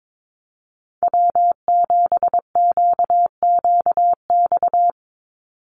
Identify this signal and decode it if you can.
Morse code, heard as a keyed tone, about 22 wpm, 700 Hz. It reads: W7QQX
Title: USA Call Signs 22wpm